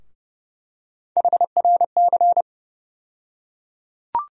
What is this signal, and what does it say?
Morse code, heard as a keyed tone, about 30 wpm, 700 Hz. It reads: HRC
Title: USA Call Sign Postfixes - 3 Letters 30wpm